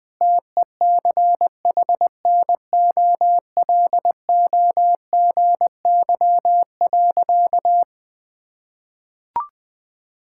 Morse code, heard as a keyed tone, about 20 wpm, 700 Hz. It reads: TECHNOLOGY.
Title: Sentences from Top 1000 Words - Review Part 5 - 20wpm